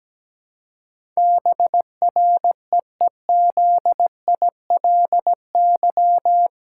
Morse code, heard as a keyed tone, about 17 wpm, 700 Hz. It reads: BREEZILY
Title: Pangram Challenge v2 - 17wpm